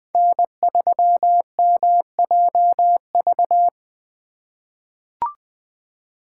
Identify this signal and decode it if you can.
Morse code, heard as a keyed tone, about 20 wpm, 700 Hz. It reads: N3MJV
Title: USA Call Signs - 5 Characters - 3x Character Spacing - 20wpm